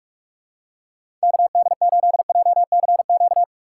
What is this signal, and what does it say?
Morse code, heard as a keyed tone, about 42 wpm, 700 Hz. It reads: KD8JCQ
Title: USA Call Signs - 6 Characters - 3x Character Spacing - 45wpm